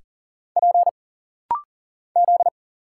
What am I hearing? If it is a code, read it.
Morse code, heard as a keyed tone, about 40 wpm, 700 Hz. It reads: P Z
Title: Single Letters - Rapid-Fire 40wpm